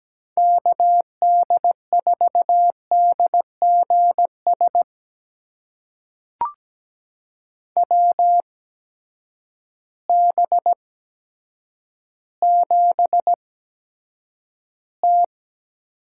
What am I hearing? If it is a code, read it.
Morse code, heard as a keyed tone, about 17 wpm, spaced as for 8 wpm, 700 Hz. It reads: KD4DGS WB7T
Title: USA Call Signs - 6 Characters - 8x Character Spacing - 17wpm